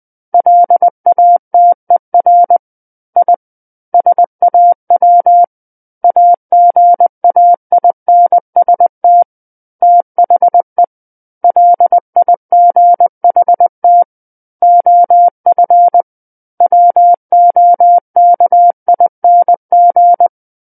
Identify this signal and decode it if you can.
Morse code, heard as a keyed tone, about 20 wpm, 700 Hz. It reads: LATER I SAW AGAINST THE LIGHT OF WOKING
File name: 20 WPM